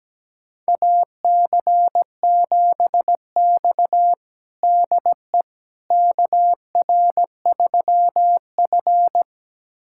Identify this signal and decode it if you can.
Morse code, heard as a keyed tone, about 17 wpm, 700 Hz. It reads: AC7X DE KR3F
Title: USA Callsign DE Callsign - 4 Characters - 17wpm